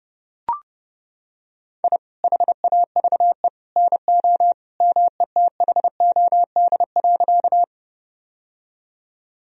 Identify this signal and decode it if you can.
Morse code, heard as a keyed tone, about 30 wpm, 700 Hz. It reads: I HAVE NO METHOD.
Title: Sentences from Top 1000 Words - Review Part 3 - 30wpm